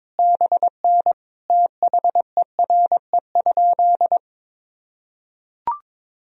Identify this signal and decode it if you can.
Morse code, heard as a keyed tone, about 22 wpm, 700 Hz. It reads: BN THERE?
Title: Sentences from Top 300 Words and Abbreviations 22wpm